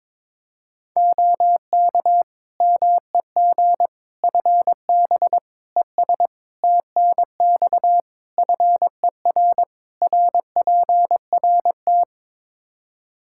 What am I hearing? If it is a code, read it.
Morse code, heard as a keyed tone, about 22 wpm, 700 Hz. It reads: OK MEG FB ES TNX FER RPRT
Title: Practice QSO - Exchange 2 - Info - 22wpm